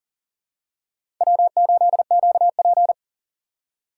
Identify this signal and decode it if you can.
Morse code, heard as a keyed tone, about 40 wpm, 700 Hz. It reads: W8QP
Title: USA Call Signs - 4 Characters - 3x Character Spacing - 40wpm